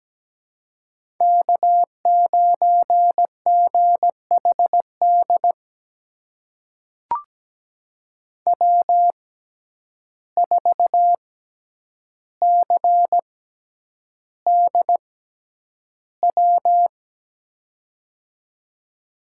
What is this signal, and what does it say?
Morse code, heard as a keyed tone, about 17 wpm, spaced as for 9 wpm, 700 Hz. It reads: K9GHD W4CDW
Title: USA Call Signs - 5 Characters - 6x Character Spacing - 17wpm